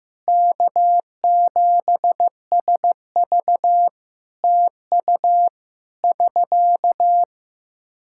Morse code, heard as a keyed tone, about 15 wpm, 700 Hz. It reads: K7SV TU <SK>
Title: K1USN SST QSO Practice 15wpm